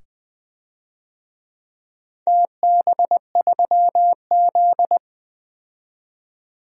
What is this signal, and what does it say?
Morse code, heard as a keyed tone, about 20 wpm, 700 Hz. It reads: TB3Z
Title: DXCC-100 Call Signs - 4 Characters - 20wpm